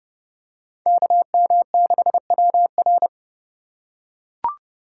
Morse code, heard as a keyed tone, about 29 wpm, 700 Hz. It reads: KM6WR
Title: USA Call Signs - 5 Characters - 2x Character Spacing - 30wpm